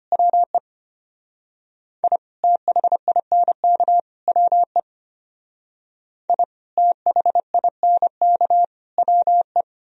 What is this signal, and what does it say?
Morse code, heard as a keyed tone, about 31 wpm, 700 Hz. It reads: WE I THINK WE I THINK WE
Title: Speed-Racing - Sets of 3 Words 25wpm